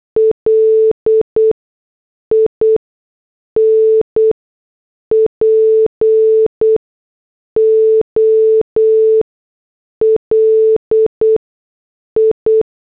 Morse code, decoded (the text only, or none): LIN POLI